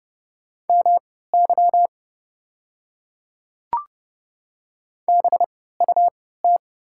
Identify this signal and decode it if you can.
Morse code, heard as a keyed tone, about 30 wpm, spaced as for 17 wpm, 700 Hz. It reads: MY BUT
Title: 3x Character Spacing - Top 100 Words - 30wpm